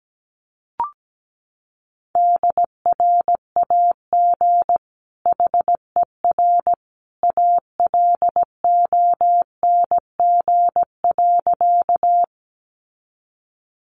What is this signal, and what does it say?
Morse code, heard as a keyed tone, about 17 wpm, 700 Hz. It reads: DRAG HER ALONG.
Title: Sentences from Top 1500 Words 17wpm